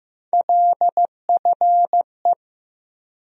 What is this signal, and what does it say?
Morse code, heard as a keyed tone, about 15 wpm, 700 Hz. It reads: LFE
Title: USA Call Sign Postfixes - 3 Letters 15wpm